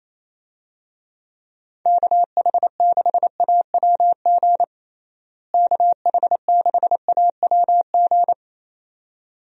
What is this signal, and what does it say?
Morse code, heard as a keyed tone, about 28 wpm, 700 Hz. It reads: KH6AWG KH6AWG
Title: DXCC-100 Call Signs - 6 Characters - Repeated 2x - 28wpm